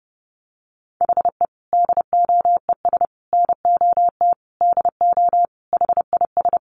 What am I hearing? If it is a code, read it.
Morse code, heard as a keyed tone, about 30 wpm, 700 Hz. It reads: HE DOES NOT DO HIS
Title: Additional Easy Phrases 30wpm